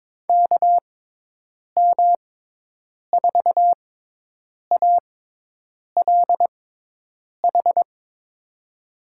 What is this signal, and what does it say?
Morse code, heard as a keyed tone, about 22 wpm, spaced as for 8 wpm, 700 Hz. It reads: KM4ALH
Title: USA Call Signs - 6 Characters - 6x Character Spacing - 22wpm